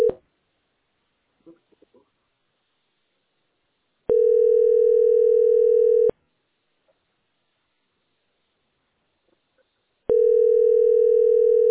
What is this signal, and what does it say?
Ringback tone (North America), 440 + 480 Hz.